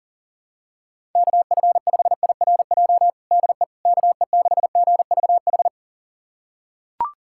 Morse code, heard as a keyed tone, about 40 wpm, 700 Hz. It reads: KF5IRJ DE KE6GVH
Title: USA Callsign DE Callsign - 6 Characters - 40wpm